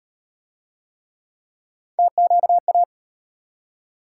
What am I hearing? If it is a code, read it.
Morse code, heard as a keyed tone, about 38 wpm, 700 Hz. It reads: TQA